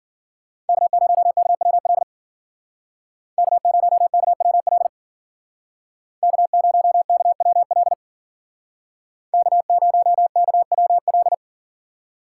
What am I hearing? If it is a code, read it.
Morse code, heard as a keyed tone, about 55 wpm, 700 Hz. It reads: K0KWL K0KWL K0KWL K0KWL